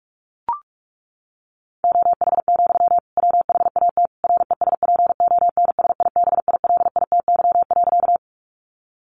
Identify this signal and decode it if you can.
Morse code, heard as a keyed tone, about 44 wpm, 700 Hz. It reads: OH, WHAT RESPONSIBILITY.
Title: Sentences from Top 1200 Words 45wpm